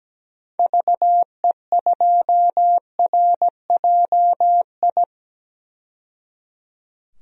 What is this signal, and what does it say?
Morse code, heard as a keyed tone, about 17 wpm, 700 Hz. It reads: VE2RJI
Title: DXCC-100 Call Signs - 6 Characters - 17wpm